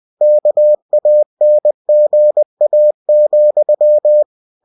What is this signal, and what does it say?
Morse code, heard as a keyed tone, about 20 wpm, 600 Hz. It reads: KANGA,
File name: cw-books / winnie-the-pooh / winnie-20wpm-page-0078